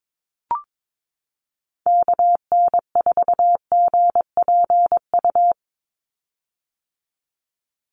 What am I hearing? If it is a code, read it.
Morse code, heard as a keyed tone, about 22 wpm, 700 Hz. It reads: KN4GPU
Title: USA Call Signs 22wpm